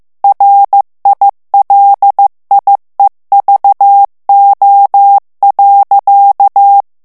Morse code, heard as a keyed tone, about 15 wpm, 800 Hz. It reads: RILIEVO.
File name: morse mp3 / ITALIEN texte clair 2 vitesse 14 mots minute